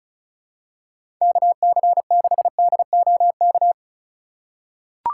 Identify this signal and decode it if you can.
Morse code, heard as a keyed tone, about 34 wpm, 700 Hz. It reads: KC6DOK
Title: USA Call Signs - 6 Characters - 3x Character Spacing - 35wpm